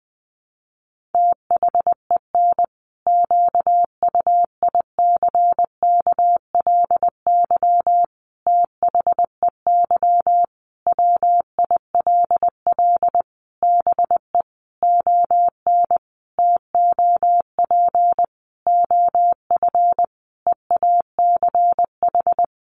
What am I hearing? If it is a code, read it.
Morse code, heard as a keyed tone, about 20 wpm, 700 Hz. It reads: THEN QUICKLY THEY WILL BE ON TOP OF EACH
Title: Sentences from Top 700 Words 20wpm